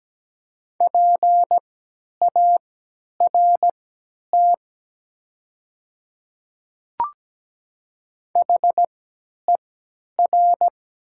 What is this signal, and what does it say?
Morse code, heard as a keyed tone, about 17 wpm, spaced as for 10 wpm, 700 Hz. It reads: PART HER
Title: 3x Character Spacing - Top 200 Words - 17wpm